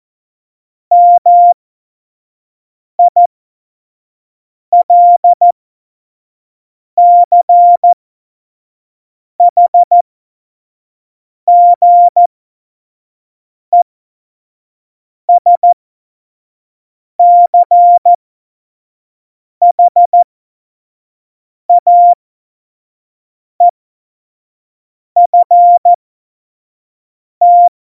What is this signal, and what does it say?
Morse code, heard as a keyed tone, about 14 wpm, spaced as for 5 wpm, 700 Hz. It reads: MILCHGESCHAEFT
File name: Morsekurs / Witze / Witz 10 April 2018, 25 F70